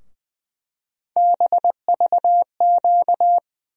Morse code, heard as a keyed tone, about 20 wpm, 700 Hz. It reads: BVQ
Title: USA Call Sign Postfixes - 3 Letters 20wpm